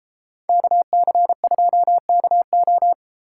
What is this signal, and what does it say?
Morse code, heard as a keyed tone, about 33 wpm, 700 Hz. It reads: KC2KO